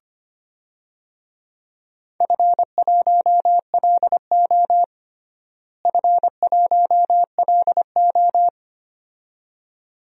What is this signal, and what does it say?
Morse code, heard as a keyed tone, about 25 wpm, 700 Hz. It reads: F1LO F1LO
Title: DXCC-100 Call Signs - 4 Characters - Repeated 2x - 25wpm